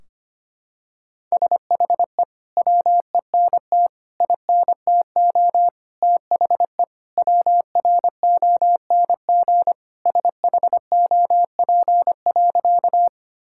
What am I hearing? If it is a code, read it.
Morse code, heard as a keyed tone, about 25 wpm, 700 Hz. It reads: SHE WENT INTO THE WRONG SHOP.
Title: Sentences from Top 1600 Words 25wpm